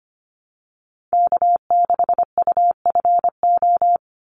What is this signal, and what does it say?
Morse code, heard as a keyed tone, about 25 wpm, 700 Hz. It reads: K6UFO
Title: Call Signs - 25wpm